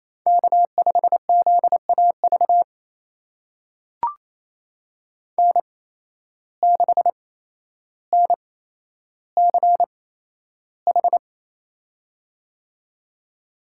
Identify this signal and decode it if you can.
Morse code, heard as a keyed tone, about 28 wpm, spaced as for 12 wpm, 700 Hz. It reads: K5ZAV N6NCH
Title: USA Call Signs - 5 Characters - 8x Character Spacing - 28wpm